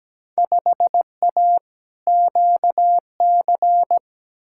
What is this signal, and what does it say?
Morse code, heard as a keyed tone, about 17 wpm, 700 Hz. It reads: 5A QC
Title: Field Day Exchanges Encore - 17wpm